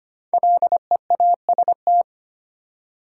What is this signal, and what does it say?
Morse code, heard as a keyed tone, about 25 wpm, 700 Hz. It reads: LEAST
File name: 5 Letter Words - First 3 Letters Given - 25wpm